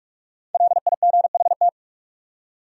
Morse code, heard as a keyed tone, about 43 wpm, 700 Hz. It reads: LIGHT